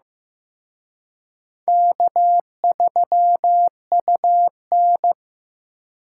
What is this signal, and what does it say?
Morse code, heard as a keyed tone, about 18 wpm, 700 Hz. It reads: K3UN
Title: Speed Racing - USA 1x2 Call Signs 15wpm